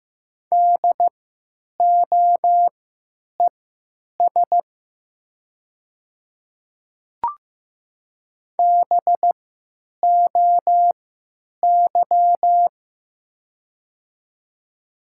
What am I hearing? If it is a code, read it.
Morse code, heard as a keyed tone, about 15 wpm, spaced as for 9 wpm, 700 Hz. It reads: DOES BOY
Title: 3x Character Spacing - Top 200 Words - 15wpm